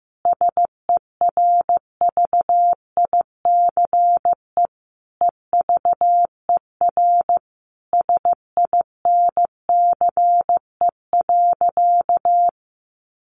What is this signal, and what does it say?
Morse code, heard as a keyed tone, about 15 wpm, 700 Hz. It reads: SERVICE EVER SINCE.
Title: Sentences from Top 700 Words 15wpm